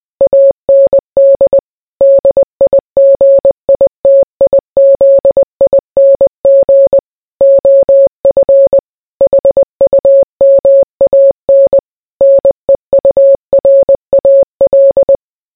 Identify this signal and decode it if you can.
Morse code, heard as a keyed tone, about 20 wpm, 550 Hz. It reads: AND DIGITIZING OF HUMAN NEURAL